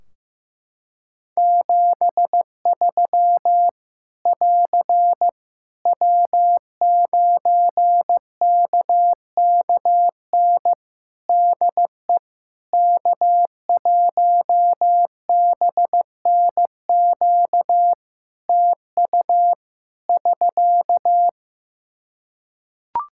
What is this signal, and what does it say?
Morse code, heard as a keyed tone, about 15 wpm, 700 Hz. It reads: 73 <AR> W9KKN DE K1BNQ TU <SK>
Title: K1USN SST QSO Practice 15wpm